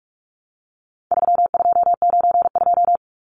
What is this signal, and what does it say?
Morse code, heard as a keyed tone, about 44 wpm, 700 Hz. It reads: 3292
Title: Numbers - 4-Digit 45wpm